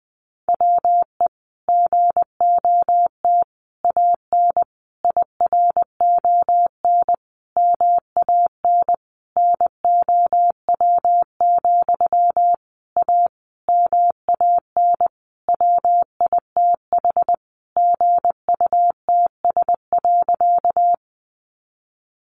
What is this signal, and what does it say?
Morse code, heard as a keyed tone, about 20 wpm, 700 Hz. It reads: WE GOT AN IRON MAN NOW, A MAN WITH GUTS.
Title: A Sound Of Thunder 20wpm